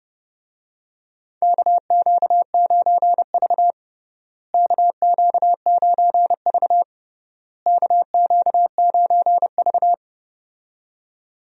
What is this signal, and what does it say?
Morse code, heard as a keyed tone, about 32 wpm, 700 Hz. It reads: KQ9V KQ9V KQ9V
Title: USA Call Signs - 4 Characters - Repeated 3x - 30wpm